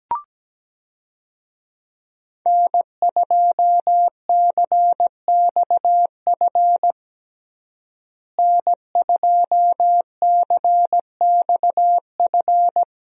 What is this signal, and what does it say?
Morse code, heard as a keyed tone, about 17 wpm, 700 Hz. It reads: N2CXF N2CXF
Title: USA Call Signs - 5 Characters - Repeated 3x - 17wpm